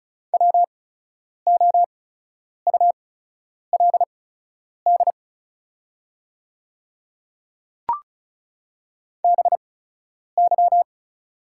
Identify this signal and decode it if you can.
Morse code, heard as a keyed tone, about 36 wpm, spaced as for 10 wpm, 700 Hz. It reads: WOULD BY
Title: 8x Character Spacing - Top 100 Words - 35wpm